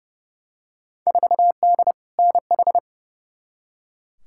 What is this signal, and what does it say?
Morse code, heard as a keyed tone, about 30 wpm, 700 Hz. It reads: 4D NH
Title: Field Day Exchanges Encore - 30wpm